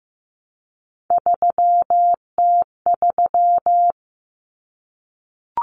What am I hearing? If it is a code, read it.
Morse code, heard as a keyed tone, about 15 wpm, 700 Hz. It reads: 3T3
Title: Common Cut Numbers - 3-Digit 15wpm